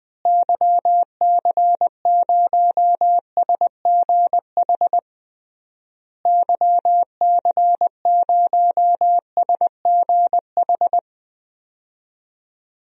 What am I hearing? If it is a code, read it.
Morse code, heard as a keyed tone, about 20 wpm, 700 Hz. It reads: YC0SGH YC0SGH
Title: DXCC-100 Call Signs - 6 Characters - Repeated 3x - 20wpm